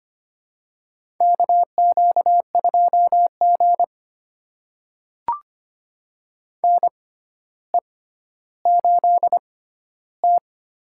Morse code, heard as a keyed tone, about 25 wpm, spaced as for 13 wpm, 700 Hz. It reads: KQ2G NE8T
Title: USA Call Signs - 4 Characters - 6x Character Spacing - 25wpm